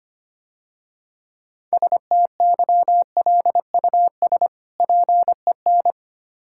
Morse code, heard as a keyed tone, about 26 wpm, 700 Hz. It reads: STYLUS PEN
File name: practice / 25wpm / Difficult Words - Speed-Racing with Hints - 25wpm